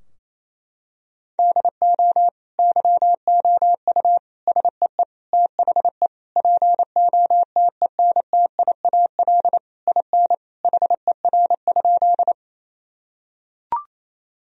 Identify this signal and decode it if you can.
Morse code, heard as a keyed tone, about 28 wpm, 700 Hz. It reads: DO YOU SEE THE POTENTIAL IN HER?
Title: Sentences from Top 1000 Words - Review Part 4 - 28wpm